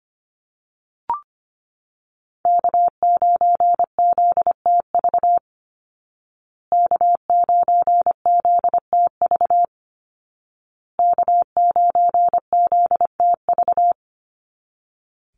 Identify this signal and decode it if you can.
Morse code, heard as a keyed tone, about 25 wpm, 700 Hz. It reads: K9ZTV K9ZTV K9ZTV
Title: Call Signs Repeated 2x - 25wpm